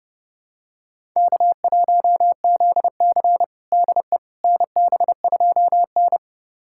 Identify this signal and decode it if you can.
Morse code, heard as a keyed tone, about 32 wpm, 700 Hz. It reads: K1ZC DE NB2N
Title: USA Callsign DE Callsign - 4 Characters - 30wpm